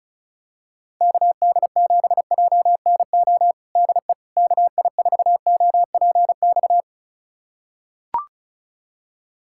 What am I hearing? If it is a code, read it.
Morse code, heard as a keyed tone, about 36 wpm, 700 Hz. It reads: KD7JNO DE KI4OPX
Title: USA Callsign DE Callsign 35wpm